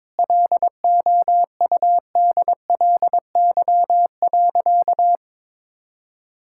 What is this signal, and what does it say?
Morse code, heard as a keyed tone, about 22 wpm, 700 Hz. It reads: LOUDLY.
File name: practice / 22wpm / Sentences with 2-Word Phrases - 22wpm